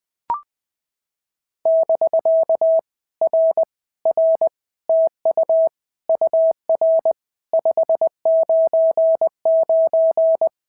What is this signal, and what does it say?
Morse code, heard as a keyed tone, about 20 wpm, 650 Hz. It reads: <BK> R R TU UR 599
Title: POTA QSO Practice - 20wpm